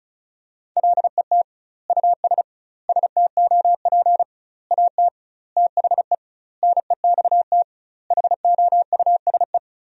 Morse code, heard as a keyed tone, about 34 wpm, 700 Hz. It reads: LET US STOP AT THE NEXT HOUSE
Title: 2x Word-Spacing - Sentences from Top 400 Words - 35wpm